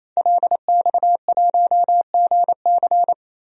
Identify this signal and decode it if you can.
Morse code, heard as a keyed tone, about 28 wpm, 700 Hz. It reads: LX1GC
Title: DXCC-100 Call Signs - 5 Characters - 28wpm